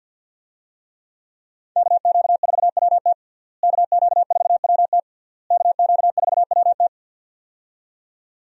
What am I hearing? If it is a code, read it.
Morse code, heard as a keyed tone, about 50 wpm, 700 Hz. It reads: KQ4WT KQ4WT KQ4WT
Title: USA Call Signs - 5 Characters - Repeated 3x - 50wpm